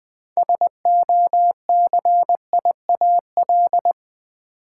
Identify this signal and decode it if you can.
Morse code, heard as a keyed tone, about 20 wpm, 690 Hz. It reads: SOCIAL
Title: Build A Word - 2 Letters At A Time - 6-Letter Words - 20wpm